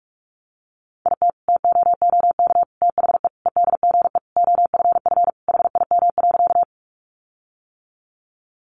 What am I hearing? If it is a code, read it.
Morse code, heard as a keyed tone, about 44 wpm, 700 Hz. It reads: IT TOOK THE EDGE OFF HIM.
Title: Sentences from Top 1300 Words 45wpm